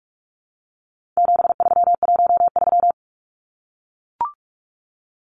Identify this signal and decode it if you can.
Morse code, heard as a keyed tone, about 44 wpm, 700 Hz. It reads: 7313
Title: Numbers - 4-Digit 45wpm